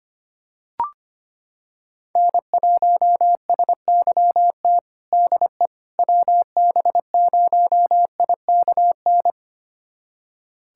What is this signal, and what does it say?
Morse code, heard as a keyed tone, about 25 wpm, 700 Hz. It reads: N1SYT DE WB0IKN
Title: USA Callsign DE Callsign 25wpm